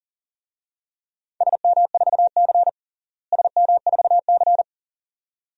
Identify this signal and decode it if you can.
Morse code, heard as a keyed tone, about 40 wpm, 700 Hz. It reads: SM4C SM4C
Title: DXCC-100 Call Signs - 4 Characters - Repeated 2x - 40wpm